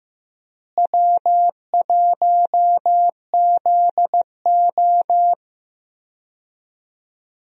Morse code, heard as a keyed tone, about 15 wpm, 700 Hz. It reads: W1ZO
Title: USA Call Signs - 4 Characters - Repeated 2x - 15wpm